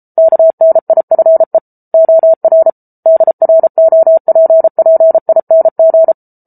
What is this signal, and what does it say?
Morse code, heard as a keyed tone, about 34 wpm, 650 Hz. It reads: KNIFE OR DROPPING